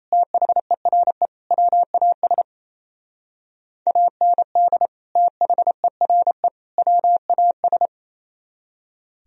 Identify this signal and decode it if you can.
Morse code, heard as a keyed tone, about 35 wpm, 700 Hz. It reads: THERE WAS AND THERE WAS